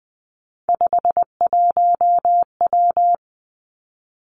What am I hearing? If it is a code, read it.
Morse code, heard as a keyed tone, about 19 wpm, 700 Hz. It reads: 51W
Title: Three Challenging Letter-Number Combinations 20wpm